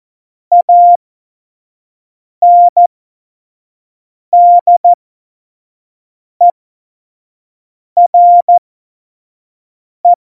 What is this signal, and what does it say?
Morse code, heard as a keyed tone, about 14 wpm, spaced as for 5 wpm, 700 Hz. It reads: ANDERE
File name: Morsekurs / Witze / Witz 16. Oktober 2018, 25 F70